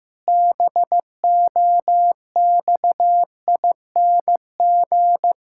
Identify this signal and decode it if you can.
Morse code, heard as a keyed tone, about 15 wpm, 700 Hz. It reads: BOXING
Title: Pangram Challenge v2 - 15wpm